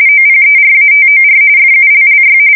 RTTY, about 45.45 baud. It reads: OPERATOR FULLY
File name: fsk170 / morkrum